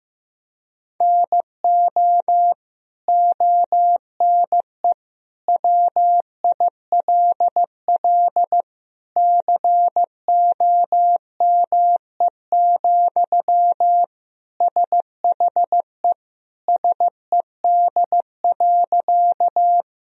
Morse code, heard as a keyed tone, about 15 wpm, 700 Hz. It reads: NO ONE WILL COME, SHE SED.
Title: Sentences from Top 100 Words and Abbreviations 15wpm